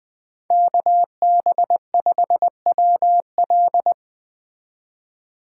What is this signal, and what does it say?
Morse code, heard as a keyed tone, about 24 wpm, 700 Hz. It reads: KB5WL
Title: Speed Racing - USA 2x2 Call Signs 20wpm